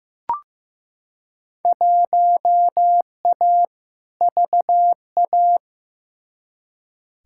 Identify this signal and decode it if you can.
Morse code, heard as a keyed tone, about 15 wpm, 700 Hz. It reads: 1A VA
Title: Field Day Exchanges Encore - 15wpm